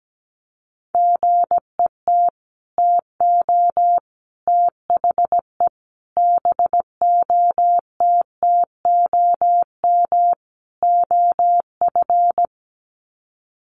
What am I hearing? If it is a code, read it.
Morse code, heard as a keyed tone, about 17 wpm, 700 Hz. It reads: GET TO THE BOTTOM OF
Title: Sets of 5 Words Encore 17wpm